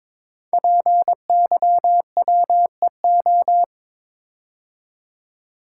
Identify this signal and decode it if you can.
Morse code, heard as a keyed tone, about 22 wpm, 700 Hz. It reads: PYWEO
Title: Code Groups - 5 Letters - 22wpm